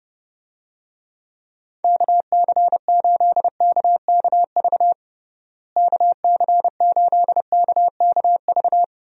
Morse code, heard as a keyed tone, about 32 wpm, 700 Hz. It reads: KC8KKV KC8KKV
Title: USA Call Signs - 6 Characters - Repeated 2x - 30wpm